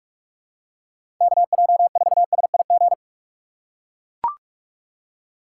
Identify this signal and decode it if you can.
Morse code, heard as a keyed tone, about 44 wpm, 700 Hz. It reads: KJ4SIG
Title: USA Call Signs - 6 Characters - 6x Character Spacing - 45wpm